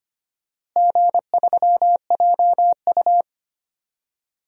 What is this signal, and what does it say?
Morse code, heard as a keyed tone, about 25 wpm, 700 Hz. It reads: G3JU
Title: DXCC-100 Call Signs - 4 Characters - Repeated 3x - 25wpm